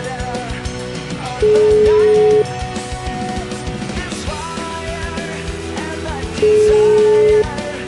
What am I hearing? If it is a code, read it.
Ringback tone (Europe, 425 Hz), with noise.